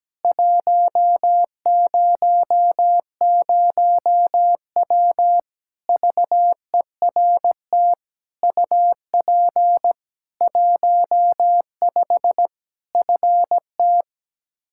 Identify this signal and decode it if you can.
Morse code, heard as a keyed tone, about 17 wpm, 700 Hz. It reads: 100W VERT UP 15 FT
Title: K1USN SST QSO Practice 17wpm